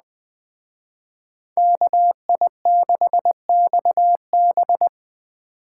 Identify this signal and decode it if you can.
Morse code, heard as a keyed tone, about 20 wpm, 700 Hz. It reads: KI6XB